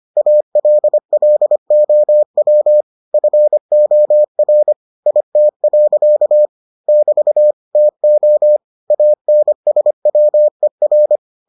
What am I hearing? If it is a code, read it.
Morse code, heard as a keyed tone, about 25 wpm, 600 Hz. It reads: ALLOW FOR IT. <BT> TO ANSWER